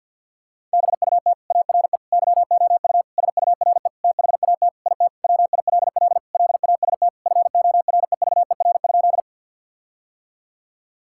Morse code, heard as a keyed tone, about 50 wpm, 700 Hz. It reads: BUT ARE YOU SURE THAT IT WILL LAST FOREVER?
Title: Sentences from Top 1600 Words 50wpm